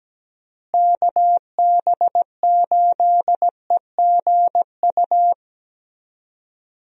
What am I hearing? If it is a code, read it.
Morse code, heard as a keyed tone, about 17 wpm, 700 Hz. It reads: KB8EGU
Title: USA Call Signs - 6 Characters - Repeated 3x - 17wpm